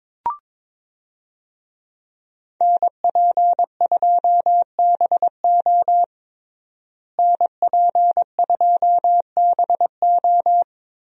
Morse code, heard as a keyed tone, about 22 wpm, 700 Hz. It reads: NP2BO NP2BO
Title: USA Call Signs - 5 Characters - Repeated 2x - 22wpm